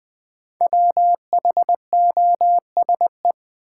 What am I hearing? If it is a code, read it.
Morse code, heard as a keyed tone, about 20 wpm, 700 Hz. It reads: WHOSE
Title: Sentences from Top 1200 Words - New Word Drill - 20wpm